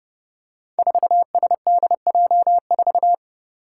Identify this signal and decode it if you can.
Morse code, heard as a keyed tone, about 30 wpm, 700 Hz. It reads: 4SDJ4
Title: Five Challenging Letter-Number Combinations 30wpm